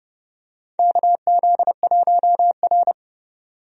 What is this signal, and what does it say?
Morse code, heard as a keyed tone, about 30 wpm, 700 Hz. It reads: KZ1R
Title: USA Call Signs - 4 Characters - 30wpm